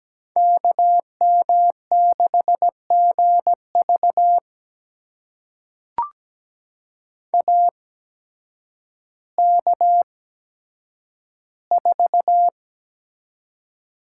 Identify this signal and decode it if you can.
Morse code, heard as a keyed tone, about 17 wpm, spaced as for 9 wpm, 700 Hz. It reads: KM6GV AK4
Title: USA Call Signs - 5 Characters - 8x Character Spacing - 17wpm